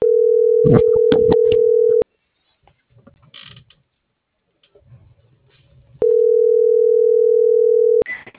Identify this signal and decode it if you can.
Ringback tone (North America), 440 + 480 Hz.